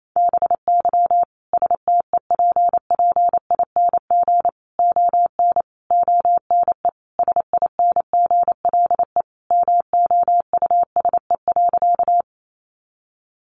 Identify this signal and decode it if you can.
Morse code, heard as a keyed tone, about 28 wpm, 700 Hz. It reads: BY STEPPING ON ONE SINGLE MOUSE.
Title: A Sound Of Thunder 28wpm